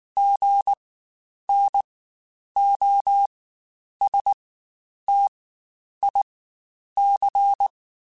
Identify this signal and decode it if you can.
Morse code, heard as a keyed tone, about 19 wpm, spaced as for 9 wpm, 800 Hz. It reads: GNOSTIC
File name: Spaced Alphas / Spaced Alphas mp3s / Gs sp @18